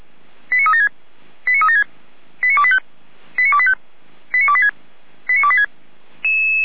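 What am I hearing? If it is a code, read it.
Five-tone selective call (ZVEI), heard as quick runs of five tones, tones of about 70 ms. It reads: 89376 89376 89285 89285 89286 89286
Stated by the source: ZVEI five-tone selective call received by SDR.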